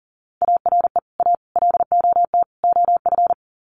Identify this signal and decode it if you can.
Morse code, heard as a keyed tone, about 40 wpm, 700 Hz. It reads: ARE A LOT OF
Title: Sets of 4 Words 40wpm